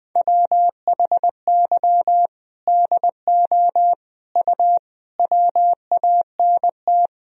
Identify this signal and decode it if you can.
Morse code, heard as a keyed tone, about 20 wpm, 700 Hz. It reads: WHY DO U WANT
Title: Sentences from Top 200 Words and Abbreviations 20wpm